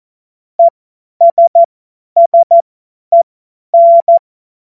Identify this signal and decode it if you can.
Morse code, heard as a keyed tone, about 14 wpm, spaced as for 10 wpm, 680 Hz. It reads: ESSEN
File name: Witz-19-August-2025-50-F70